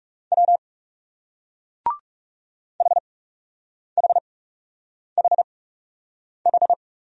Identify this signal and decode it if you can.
Morse code, heard as a keyed tone, about 40 wpm, 700 Hz. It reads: W H H H H